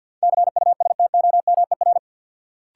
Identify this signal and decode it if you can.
Morse code, heard as a keyed tone, about 50 wpm, 700 Hz. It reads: CUSTOMER
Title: Sentences from Top 1100 Words - New Word Drill - 50wpm